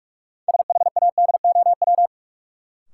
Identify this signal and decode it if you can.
Morse code, heard as a keyed tone, about 43 wpm, 700 Hz. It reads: SHADOW
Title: Sets of 5 Words Encore - New Word Drill - 45wpm